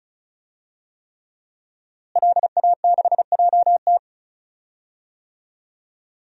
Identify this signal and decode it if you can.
Morse code, heard as a keyed tone, about 36 wpm, 700 Hz. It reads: LA6JT
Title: DXCC-100 Call Signs - 5 Characters - Repeated 3x - 35wpm